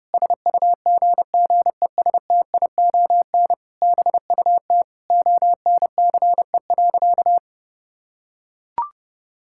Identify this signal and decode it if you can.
Morse code, heard as a keyed tone, about 32 wpm, 700 Hz. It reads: SUGGESTION BUT ONCE.
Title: Sentences from Top 1600 Words 30wpm